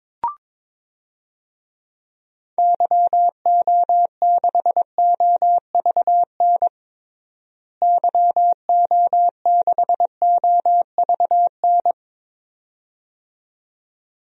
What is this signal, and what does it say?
Morse code, heard as a keyed tone, about 22 wpm, 700 Hz. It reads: YO6OVN YO6OVN
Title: DXCC-100 Call Signs - 6 Characters - Repeated 2x - 22wpm